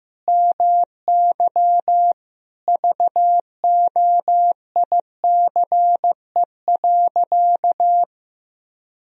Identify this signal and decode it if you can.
Morse code, heard as a keyed tone, about 15 wpm, 700 Hz. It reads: MY VOICE.
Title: Sentences from Top 1000 Words - Review Part 4 - 15wpm